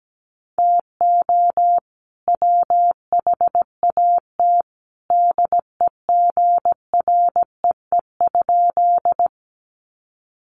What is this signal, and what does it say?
Morse code, heard as a keyed tone, about 17 wpm, 700 Hz. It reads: TO WHAT DEGREE?
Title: Sentences from Top 1500 Words 17wpm